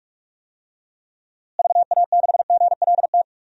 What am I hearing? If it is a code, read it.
Morse code, heard as a keyed tone, about 46 wpm, 690 Hz. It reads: VA6GLT
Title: DXCC-100 Call Signs - 6 Characters - Repeated 2x - 45wpm